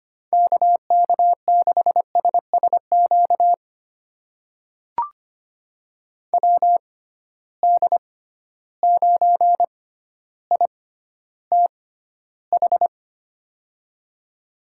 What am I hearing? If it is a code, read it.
Morse code, heard as a keyed tone, about 25 wpm, spaced as for 13 wpm, 700 Hz. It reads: KK6SSQ WD9ITH
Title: USA Call Signs - 6 Characters - 6x Character Spacing - 25wpm